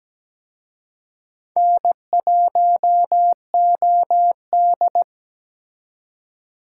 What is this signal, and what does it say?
Morse code, heard as a keyed tone, about 20 wpm, 700 Hz. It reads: N1OD
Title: Speed Racing - USA 1x2 Call Signs 17wpm